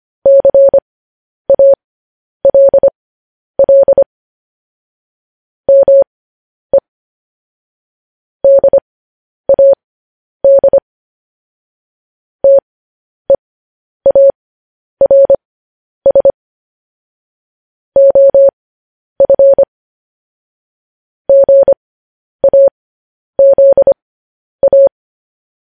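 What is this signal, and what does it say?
Morse code, heard as a keyed tone, about 25 wpm, spaced as for 10 wpm, 550 Hz. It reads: CALL ME DAD TEARS OF GAZA
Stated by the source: Morse code news headlines broadcast.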